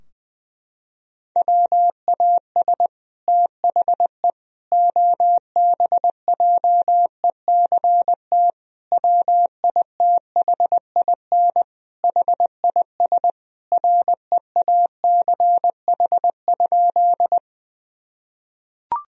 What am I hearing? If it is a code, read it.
Morse code, heard as a keyed tone, about 21 wpm, 700 Hz. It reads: WAS THE OBJECT WITHIN HIS REACH?
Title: Sentences from Top 1000 Words - Review Part 4 - 20wpm